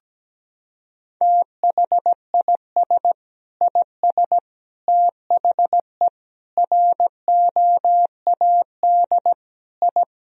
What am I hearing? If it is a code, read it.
Morse code, heard as a keyed tone, about 17 wpm, 700 Hz. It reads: THIS IS THE ROAD I
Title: Sentences from Top 1000 Words - Review Part 2 - 17wpm